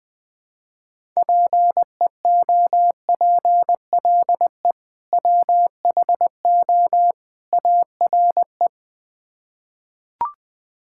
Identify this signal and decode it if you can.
Morse code, heard as a keyed tone, about 20 wpm, 700 Hz. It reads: PEOPLE WHO ARE
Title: Build A Phrase - 3-Word Phrases - 20wpm